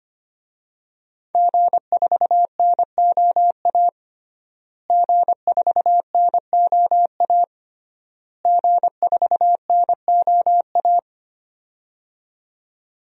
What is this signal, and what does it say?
Morse code, heard as a keyed tone, about 25 wpm, 700 Hz. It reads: G4NOA G4NOA G4NOA
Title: DXCC-100 Call Signs - 5 Characters - Repeated 3x - 25wpm